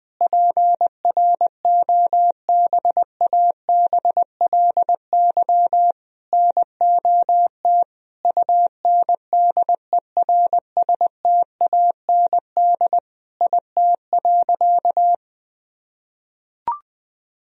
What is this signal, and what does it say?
Morse code, heard as a keyed tone, about 20 wpm, 700 Hz. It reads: PROBABLY NOT UNDERSTAND IT.
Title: Sentences from Top 1000 Words - Review Part 4 - 20wpm